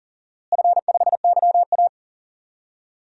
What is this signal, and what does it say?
Morse code, heard as a keyed tone, about 40 wpm, 700 Hz. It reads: F5YA
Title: DXCC-100 Call Signs - 4 Characters - Repeated 3x - 40wpm